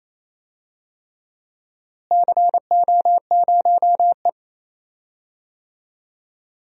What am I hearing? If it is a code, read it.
Morse code, heard as a keyed tone, about 28 wpm, 700 Hz. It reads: CO0E
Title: DXCC-100 Call Signs - 4 Characters - Repeated 3x - 28wpm